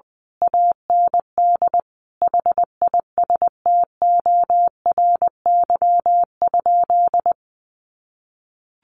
Morse code, heard as a keyed tone, about 20 wpm, 700 Hz. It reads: AND HISTORY?
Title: Sentences from Top 600 Words 20wpm